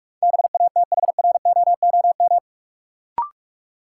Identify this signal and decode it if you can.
Morse code, heard as a keyed tone, about 46 wpm, 700 Hz. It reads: BATHROOM
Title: Sets of 5 Words Encore - New Word Drill - 45wpm